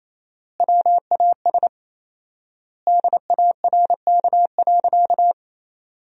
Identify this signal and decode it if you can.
Morse code, heard as a keyed tone, about 28 wpm, 700 Hz. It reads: WAS DARK.
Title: 4x Word-Spacing - Sentences from Top 500 Words - 28wpm